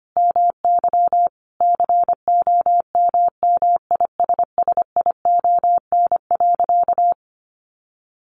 Morse code, heard as a keyed tone, about 25 wpm, 700 Hz. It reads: MY COMMISSION.
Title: Sentences from Top 1400 Words 25wpm